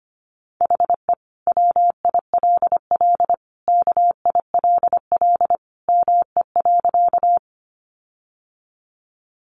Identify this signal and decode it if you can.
Morse code, heard as a keyed tone, about 25 wpm, 700 Hz. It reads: HE WILL KILL ME.
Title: Sentences from Top 1400 Words 25wpm